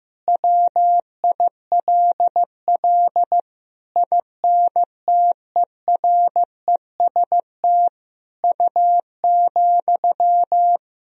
Morse code, heard as a keyed tone, about 15 wpm, 700 Hz. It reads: WILL INTEREST U,
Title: Sentences from Top 400 Words and Abbreviations 15wpm